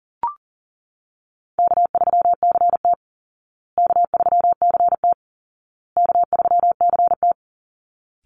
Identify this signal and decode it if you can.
Morse code, heard as a keyed tone, about 40 wpm, 700 Hz. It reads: K3CT K3CT K3CT
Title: Easy Call Signs Repeated 3x - 40wpm